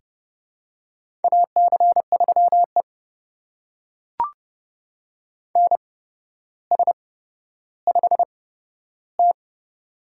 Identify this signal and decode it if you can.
Morse code, heard as a keyed tone, about 30 wpm, spaced as for 13 wpm, 700 Hz. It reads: AC3E NS5T
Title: USA Call Signs - 4 Characters - 8x Character Spacing - 30wpm